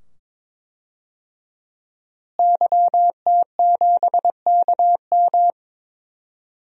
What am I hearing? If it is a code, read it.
Morse code, heard as a keyed tone, about 21 wpm, 700 Hz. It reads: YT7KM
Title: DXCC-100 Call Signs - 5 Characters - Repeated 2x - 22wpm